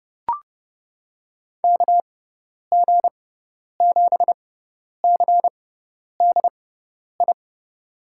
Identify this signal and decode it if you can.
Morse code, heard as a keyed tone, about 30 wpm, spaced as for 10 wpm, 700 Hz. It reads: KG7CDI
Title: USA Call Signs - 6 Characters - 6x Character Spacing - 30wpm